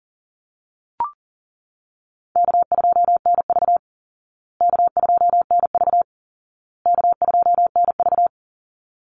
Morse code, heard as a keyed tone, about 40 wpm, 700 Hz. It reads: K2NV K2NV K2NV
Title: Call Signs Repeated 2x - 40wpm